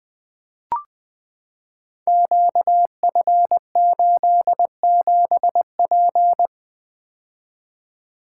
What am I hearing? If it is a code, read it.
Morse code, heard as a keyed tone, about 20 wpm, 700 Hz. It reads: QF87P
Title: Code Groups - 5 Letter:Numbers - 20wpm